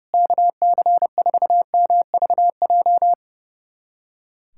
Morse code, heard as a keyed tone, about 30 wpm, 700 Hz. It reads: KC4MVJ
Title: USA Callsign DE Callsign - 6 Characters - 30wpm